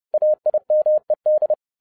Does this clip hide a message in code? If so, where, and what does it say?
Yes: Morse code from 0.1 to 1.5 s, heard as a keyed tone, about 30 wpm, 610 Hz. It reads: AIMED